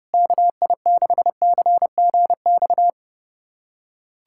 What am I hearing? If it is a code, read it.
Morse code, heard as a keyed tone, about 32 wpm, 700 Hz. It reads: KI6CGX
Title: USA Callsign DE Callsign 30wpm